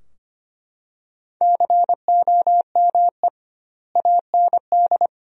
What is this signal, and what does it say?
Morse code, heard as a keyed tone, about 25 wpm, 700 Hz. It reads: COME AND
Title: 2x Word-Spacing - Sentences from Top 200 Words - 25wpm